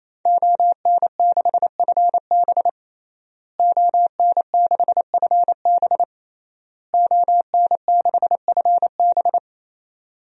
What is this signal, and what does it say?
Morse code, heard as a keyed tone, about 28 wpm, 700 Hz. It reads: ON6FB ON6FB ON6FB
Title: DXCC-100 Call Signs - 5 Characters - Repeated 3x - 28wpm